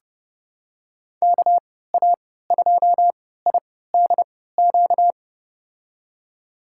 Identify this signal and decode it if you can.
Morse code, heard as a keyed tone, about 30 wpm, spaced as for 17 wpm, 700 Hz. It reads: KA2IDQ
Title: USA Call Signs - 6 Characters - 3x Character Spacing - 30wpm